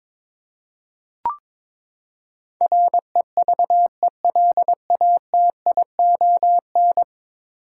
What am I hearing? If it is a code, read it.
Morse code, heard as a keyed tone, about 22 wpm, 700 Hz. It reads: REVELATION
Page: morse-code-ninja-practice-22wpm